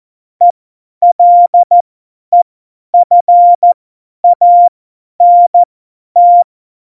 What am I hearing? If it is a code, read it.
Morse code, heard as a keyed tone, about 14 wpm, spaced as for 10 wpm, 700 Hz. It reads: ELEFANT
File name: Morsekurs / Witze / Witz 11. Dezember, 50 F70